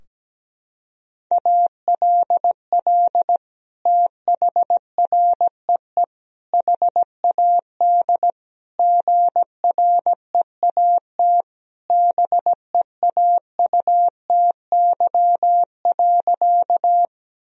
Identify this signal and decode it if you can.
Morse code, heard as a keyed tone, about 17 wpm, 700 Hz. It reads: ALL THREE HAD GREAT BEAUTY.
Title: Sentences from Top 1000 Words - Review Part 2 - 17wpm